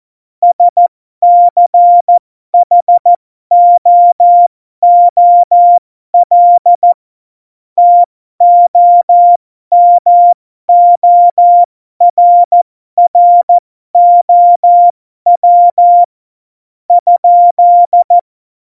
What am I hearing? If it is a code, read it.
Morse code, heard as a keyed tone, about 14 wpm, spaced as for 12 wpm, 700 Hz. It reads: SCHOOL TOMORROW ?
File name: Morsekurs / Witze